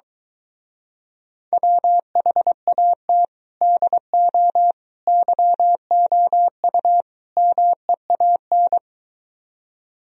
Morse code, heard as a keyed tone, about 23 wpm, 700 Hz. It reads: WHAT DO YOU MEAN